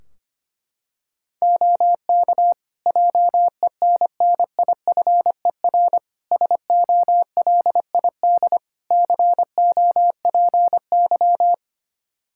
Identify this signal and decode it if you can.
Morse code, heard as a keyed tone, about 25 wpm, 700 Hz. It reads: OK JENNIFER SOLID COPY
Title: Practice QSO - Exchange 3 - More Info - 25wpm